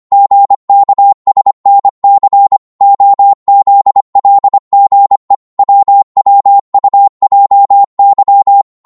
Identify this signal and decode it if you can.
Morse code, heard as a keyed tone, about 25 wpm, 800 Hz. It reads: GKSNC OZLGE WWUJY